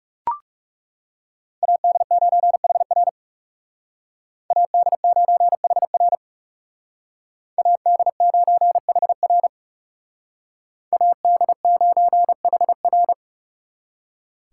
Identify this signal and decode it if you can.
Morse code, heard as a keyed tone, about 40 wpm, 700 Hz. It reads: AD9HR AD9HR AD9HR AD9HR